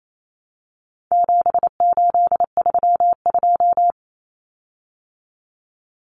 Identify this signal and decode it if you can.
Morse code, heard as a keyed tone, about 28 wpm, 700 Hz. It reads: 7832
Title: Common Cut Numbers - 4-Digit 28wpm